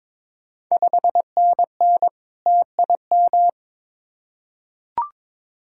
Morse code, heard as a keyed tone, about 22 wpm, 690 Hz. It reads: 5NN TIM
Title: Ontario QSO Party Exchange 22wpm